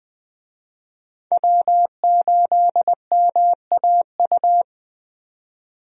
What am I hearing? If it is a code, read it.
Morse code, heard as a keyed tone, about 20 wpm, 700 Hz. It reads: W8MAU
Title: USA Call Signs - 5 Characters - 20wpm